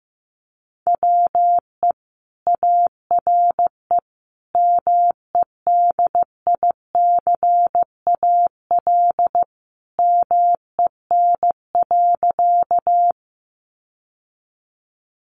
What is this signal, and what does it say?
Morse code, heard as a keyed tone, about 15 wpm, 700 Hz. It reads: WE ARE MEDICAL MEN.
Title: Sentences from Top 800 Words 15wpm